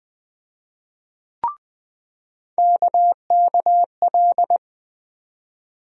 Morse code, heard as a keyed tone, about 20 wpm, 700 Hz. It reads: KKL
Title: USA Call Sign Postfixes - 3 Letters 20wpm